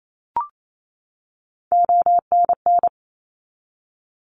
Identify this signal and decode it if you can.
Morse code, heard as a keyed tone, about 28 wpm, 700 Hz. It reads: ONN
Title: ARRL Sections 28wpm